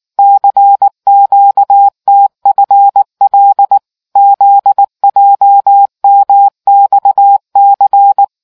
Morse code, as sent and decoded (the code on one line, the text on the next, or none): -.-. --.- - ..-. .-.. / --.. .--- -- -..- -.-.
CQTFL ZJMXC